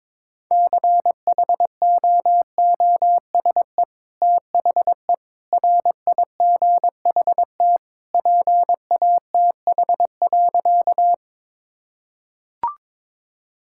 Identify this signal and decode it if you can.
Morse code, heard as a keyed tone, about 22 wpm, 700 Hz. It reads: CHOOSE THE RIGHT PATH.
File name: practice / 22wpm / Sentences with 2-Word Phrases - 22wpm